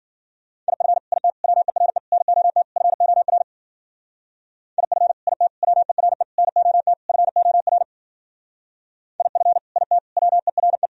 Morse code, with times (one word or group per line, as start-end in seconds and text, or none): 0.7-1.0 IF
1.1-1.3 IT
1.4-2.0 WERE
2.1-2.6 NOT
2.8-3.4 FOR
4.8-5.1 IF
5.3-5.5 IT
5.6-6.2 WERE
6.4-6.9 NOT
7.1-7.8 FOR
9.2-9.6 IF
9.8-10.0 IT
10.2-10.9 WERE